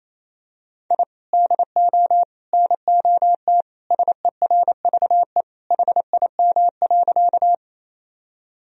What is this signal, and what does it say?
Morse code, heard as a keyed tone, about 28 wpm, 700 Hz. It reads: I DO NOT SERVE HIM.
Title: Sentences from Top 1000 Words - Review Part 2 - 28wpm